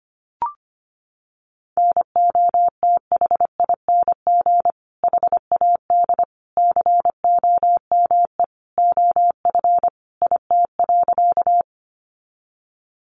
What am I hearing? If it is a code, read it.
Morse code, heard as a keyed tone, about 25 wpm, 700 Hz. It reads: NOTHING HAD COME OF IT.
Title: Sentences from Top 500 Words 25wpm